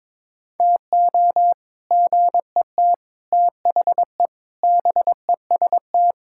Morse code, heard as a keyed tone, about 22 wpm, 700 Hz. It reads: TO GET THE BEST
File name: practice / 22wpm / Sentences with 3-Word Phrases - 22wpm